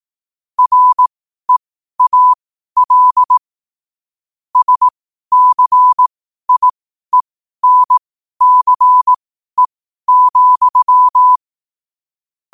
Morse code, heard as a keyed tone, about 18 wpm, spaced as for 12 wpm, 1000 Hz. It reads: REAL SCIENCE,
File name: qotdfiles / 12wpm